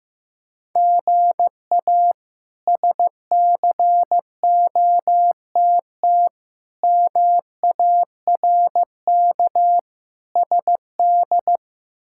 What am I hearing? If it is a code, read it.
Morse code, heard as a keyed tone, about 15 wpm, 700 Hz. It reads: GA SCOTT MARK SD
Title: K1USN SST QSO Practice - 15wpm